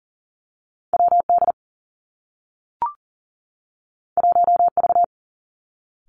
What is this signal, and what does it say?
Morse code, heard as a keyed tone, about 42 wpm, 700 Hz. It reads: W D 1 V
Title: Two Challenging Letter-Number Combinations 40wpm